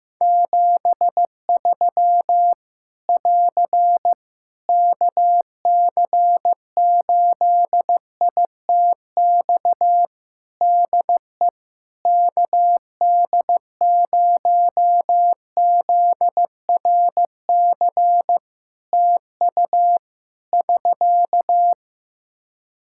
Morse code, heard as a keyed tone, about 15 wpm, 700 Hz. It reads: 73 <AR> KC8ITX DE KD0ZRC TU <SK>
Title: Practice QSO - Exchange 4 - Ending - 15wpm